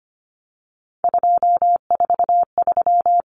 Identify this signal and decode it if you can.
Morse code, heard as a keyed tone, about 25 wpm, 700 Hz. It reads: 243
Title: Common Cut Numbers - 3-Digit 25wpm